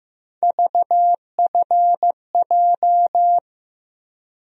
Morse code, heard as a keyed tone, about 15 wpm, 700 Hz. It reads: VFJ
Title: USA Call Sign Postfixes - 3 Letters 15wpm